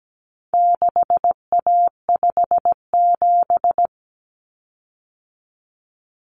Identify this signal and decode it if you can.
Morse code, heard as a keyed tone, about 17 wpm, 700 Hz. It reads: 6A57
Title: Common Cut Numbers - 4-Digit 17wpm